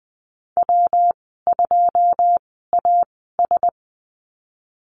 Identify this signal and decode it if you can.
Morse code, heard as a keyed tone, about 20 wpm, spaced as for 15 wpm, 700 Hz. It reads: W2AS
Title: 128 Lesson 21g - Number 6 - Call Signs 20wpm